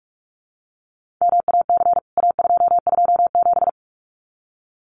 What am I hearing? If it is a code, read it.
Morse code, heard as a keyed tone, about 44 wpm, 700 Hz. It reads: MAC A227
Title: CWT Exchange 45wpm